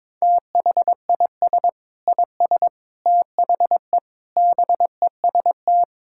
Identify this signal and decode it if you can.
Morse code, heard as a keyed tone, about 22 wpm, 700 Hz. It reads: THIS IS THE BEST
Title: Sentences from Top 1000 Words - Review Part 6 - 22wpm